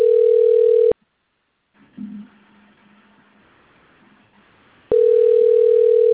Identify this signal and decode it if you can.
Ringback tone (North America), 440 + 480 Hz.
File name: rg-1600-unknown- - - .43654